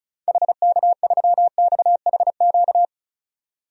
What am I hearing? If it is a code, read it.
Morse code, heard as a keyed tone, about 36 wpm, 700 Hz. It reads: HK3XHQ
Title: DXCC-100 Call Signs - 6 Characters - 35wpm